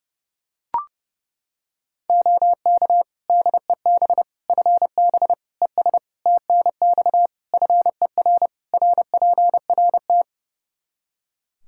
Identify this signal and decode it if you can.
Morse code, heard as a keyed tone, about 30 wpm, 700 Hz. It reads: OK DEB FB ES TNX FER RPRT
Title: Practice QSO - Exchange 2 - Info - 30wpm